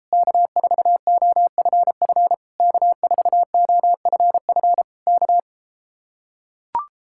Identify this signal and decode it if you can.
Morse code, heard as a keyed tone, about 33 wpm, 700 Hz. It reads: K4OFF K4OFF K